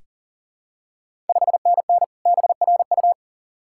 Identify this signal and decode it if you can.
Morse code, heard as a keyed tone, about 39 wpm, 700 Hz. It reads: 5NN BRU
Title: Ontario QSO Party Exchange 40wpm